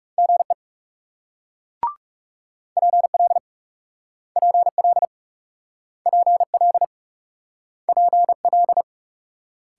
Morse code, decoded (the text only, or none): GE PL PL PL PL